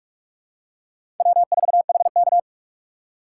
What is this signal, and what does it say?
Morse code, heard as a keyed tone, about 46 wpm, 700 Hz. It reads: W4HK
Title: USA Call Signs - 4 Characters - 2x Character Spacing - 45wpm